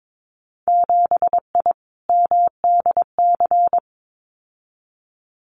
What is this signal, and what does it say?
Morse code, heard as a keyed tone, about 22 wpm, 700 Hz. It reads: 7I MDC
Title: Winter Field Day Exchanges Encore 22wpm